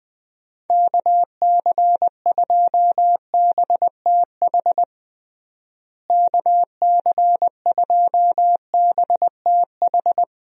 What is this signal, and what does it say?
Morse code, heard as a keyed tone, about 20 wpm, 700 Hz. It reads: KC2BTH KC2BTH
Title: USA Call Signs - 6 Characters - Repeated 2x - 20wpm